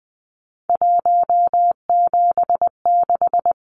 Morse code, heard as a keyed tone, about 20 wpm, 700 Hz. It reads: 176
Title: Numbers - 3-Digit 20wpm